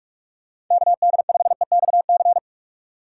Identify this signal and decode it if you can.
Morse code, heard as a keyed tone, about 44 wpm, 700 Hz. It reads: KD5EXC
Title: USA Call Signs - 6 Characters - 2x Character Spacing - 45wpm